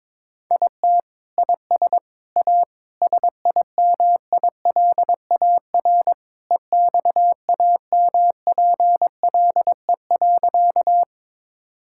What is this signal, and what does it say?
Morse code, heard as a keyed tone, about 21 wpm, 700 Hz. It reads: IT IS A SIMILAR EXAMPLE.
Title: Sentences from Top 1000 Words - Review Part 5 - 22wpm